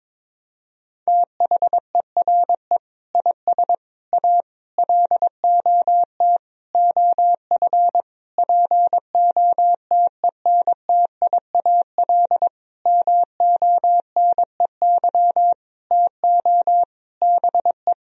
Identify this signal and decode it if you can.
Morse code, heard as a keyed tone, about 22 wpm, 700 Hz. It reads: THERE IS A LOT OF POTENTIAL MONEY TO BE
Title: Sentences from Top 1000 Words - Review Part 4 - 22wpm